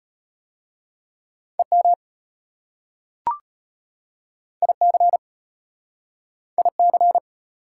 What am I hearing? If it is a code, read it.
Morse code, heard as a keyed tone, about 36 wpm, 700 Hz. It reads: EM IC IC